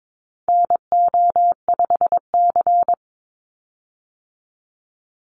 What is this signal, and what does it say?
Morse code, heard as a keyed tone, about 22 wpm, 700 Hz. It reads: NO5C
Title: USA Call Signs 22wpm